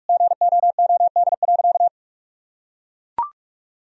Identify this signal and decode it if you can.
Morse code, heard as a keyed tone, about 42 wpm, 700 Hz. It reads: GOOD.
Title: Sentences from Top 1000 Words - Review Part 4 - 45wpm